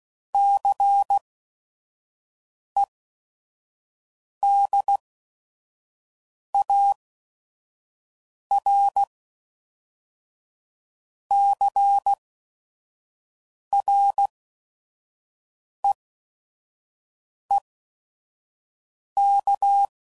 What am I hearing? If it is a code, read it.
Morse code, heard as a keyed tone, about 16 wpm, spaced as for 5 wpm, 800 Hz. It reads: CEDAR CREEK
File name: CW Code Groups @5WPM / fromSM's / Cities